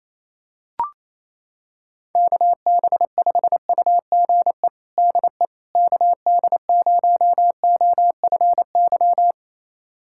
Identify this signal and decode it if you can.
Morse code, heard as a keyed tone, about 28 wpm, 700 Hz. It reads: KB5UGE DE KD0OFY
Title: USA Callsign DE Callsign 28wpm